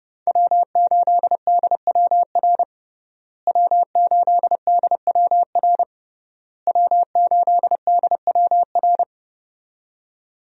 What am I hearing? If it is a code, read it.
Morse code, heard as a keyed tone, about 30 wpm, 700 Hz. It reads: W8DWR W8DWR W8DWR
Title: USA Call Signs - 5 Characters - Repeated 3x - 30wpm